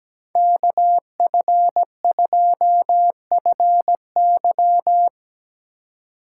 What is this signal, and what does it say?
Morse code, heard as a keyed tone, about 19 wpm, 700 Hz. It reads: KF2FY
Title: Speed Racing - USA 2x2 Call Signs 17wpm